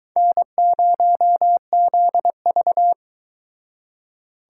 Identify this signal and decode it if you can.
Morse code, heard as a keyed tone, about 22 wpm, 700 Hz. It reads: N0ZV